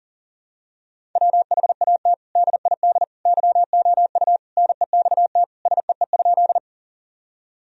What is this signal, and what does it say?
Morse code, heard as a keyed tone, about 37 wpm, 700 Hz. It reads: WHAT DID YOU NEXT SEE?
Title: Sentences from Top 1000 Words - Review Part 1 - 40wpm